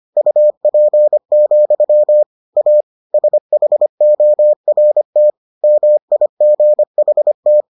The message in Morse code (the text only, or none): UP, A SHORT MIGHT